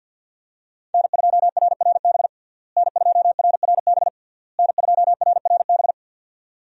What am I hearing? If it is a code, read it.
Morse code, heard as a keyed tone, about 51 wpm, 700 Hz. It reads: N2RRB N2RRB N2RRB
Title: USA Call Signs - 5 Characters - Repeated 3x - 50wpm